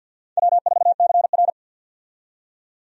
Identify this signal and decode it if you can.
Morse code, heard as a keyed tone, about 52 wpm, 700 Hz. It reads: W4CR
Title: USA Call Signs - 4 Characters - 50wpm